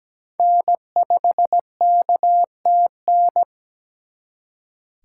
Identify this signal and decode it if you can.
Morse code, heard as a keyed tone, about 20 wpm, 700 Hz. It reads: N5KTN
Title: Speed Racing - USA 1x3 Call Signs 17wpm